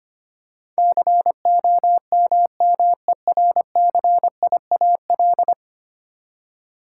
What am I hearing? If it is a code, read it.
Morse code, heard as a keyed tone, about 25 wpm, 700 Hz. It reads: COMMERCIAL
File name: 10 Letter Words - First 3 Letters Given - 25wpm